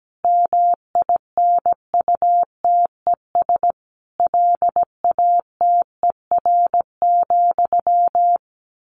Morse code, heard as a keyed tone, about 17 wpm, 700 Hz. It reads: MINUTES LATER,
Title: A Sound Of Thunder 17wpm